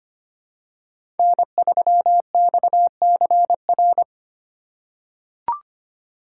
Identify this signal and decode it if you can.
Morse code, heard as a keyed tone, about 25 wpm, 700 Hz. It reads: N3XCR
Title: USA Call Signs - 5 Characters - 2x Character Spacing - 25wpm